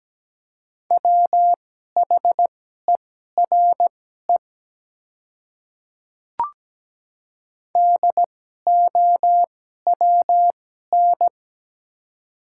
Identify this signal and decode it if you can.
Morse code, heard as a keyed tone, about 17 wpm, spaced as for 12 wpm, 700 Hz. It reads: WHERE DOWN
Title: 2x Character Spacing - Top 200 Words - 17wpm